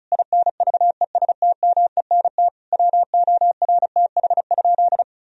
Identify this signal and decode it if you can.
Morse code, heard as a keyed tone, about 36 wpm, 700 Hz. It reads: INVESTMENT WORTH?
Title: Sentences from Top 1000 Words - Review Part 4 - 35wpm